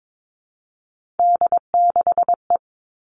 Morse code, heard as a keyed tone, about 22 wpm, 700 Hz. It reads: D6E
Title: All Cut Numbers - 3-Digit 22wpm